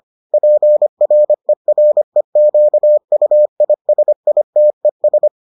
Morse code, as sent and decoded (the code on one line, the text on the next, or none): .--. .-. .-. --.- ..- .. ... .. - . ...
PREREQUISITES